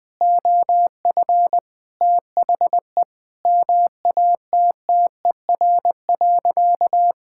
Morse code, heard as a keyed tone, about 21 wpm, 700 Hz. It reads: OF THE MATTER.
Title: Sentences with 4-Word Phrases - 20wpm